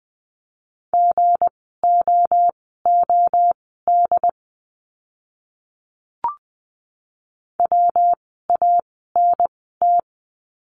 Morse code, heard as a keyed tone, about 20 wpm, spaced as for 15 wpm, 700 Hz. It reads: GOOD WANT
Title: 160 Lesson 26f - Letter g - Words 20wpm